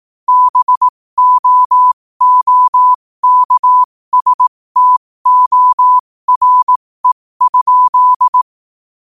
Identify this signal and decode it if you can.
Morse code, heard as a keyed tone, about 18 wpm, spaced as for 15 wpm, 1000 Hz. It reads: BOOKSTORE?
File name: qotdfiles / 15wpm